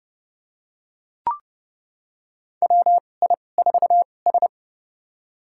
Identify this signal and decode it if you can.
Morse code, heard as a keyed tone, about 30 wpm, spaced as for 22 wpm, 700 Hz. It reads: WI4S
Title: USA Call Signs - 4 Characters - 2x Character Spacing - 30wpm